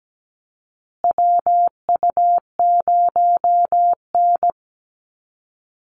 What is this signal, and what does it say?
Morse code, heard as a keyed tone, about 17 wpm, 700 Hz. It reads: WU0N
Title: USA Call Signs Repeated 2x 17wpm